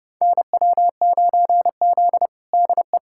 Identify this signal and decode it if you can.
Morse code, heard as a keyed tone, about 30 wpm, 700 Hz. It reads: NW9Z DE
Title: USA Callsign DE Callsign - 4 Characters - 30wpm